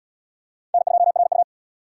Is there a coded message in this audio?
Yes: Morse code from 0.7 to 1.4 s, heard as a keyed tone, about 70 wpm, 700 Hz. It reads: N2RU